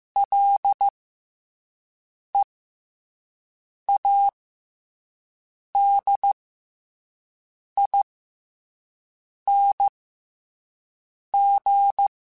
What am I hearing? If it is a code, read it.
Morse code, heard as a keyed tone, about 15 wpm, spaced as for 5 wpm, 800 Hz. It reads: LEADING
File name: F and L / F and L mp3 / F and L Prefixes 3x2 Twice @5